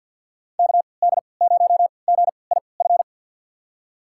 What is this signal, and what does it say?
Morse code, heard as a keyed tone, about 52 wpm, spaced as for 29 wpm, 700 Hz. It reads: KD0GIF
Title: USA Call Signs - 6 Characters - 3x Character Spacing - 50wpm